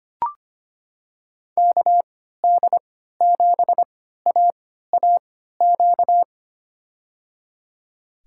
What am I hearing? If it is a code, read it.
Morse code, heard as a keyed tone, about 25 wpm, spaced as for 14 wpm, 690 Hz. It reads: KD7AAQ
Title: USA Call Signs - 6 Characters - 3x Character Spacing - 25wpm